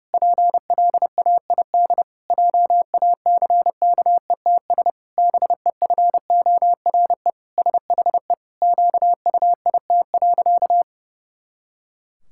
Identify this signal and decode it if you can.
Morse code, heard as a keyed tone, about 32 wpm, 700 Hz. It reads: PLAID JACKETS BEFORE SHE QUIT.
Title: Pangram Challenge v2 - 30wpm